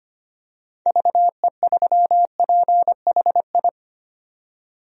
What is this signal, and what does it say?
Morse code, heard as a keyed tone, about 25 wpm, 690 Hz. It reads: VE3PHI
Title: DXCC-100 Call Signs - 6 Characters - 25wpm